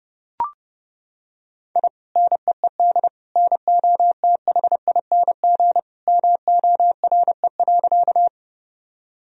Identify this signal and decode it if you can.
Morse code, heard as a keyed tone, about 32 wpm, 700 Hz. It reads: I NEED NOTHING MORE.
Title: Sentences from Top 1000 Words - Review Part 1 - 30wpm